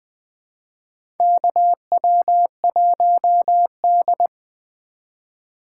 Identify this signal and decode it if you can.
Morse code, heard as a keyed tone, about 20 wpm, 700 Hz. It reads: KW1D
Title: USA Call Signs - 4 Characters - 3x Character Spacing - 20wpm